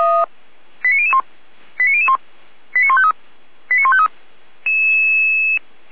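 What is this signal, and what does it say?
Five-tone selective call (ZVEI), heard as quick runs of five tones, tones of about 70 ms. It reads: 89001 89001 89153 89153
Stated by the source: ZVEI five-tone selective call received on SDR.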